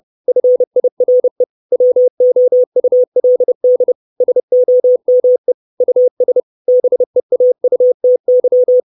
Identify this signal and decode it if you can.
Morse code, heard as a keyed tone, about 30 wpm, 500 Hz. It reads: FIRE WOULD SOME US BEAUTY
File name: 632 Words 30WPM